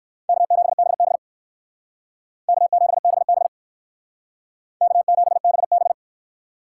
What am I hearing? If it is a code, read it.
Morse code, heard as a keyed tone, about 57 wpm, 700 Hz. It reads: K7BB K7BB K7BB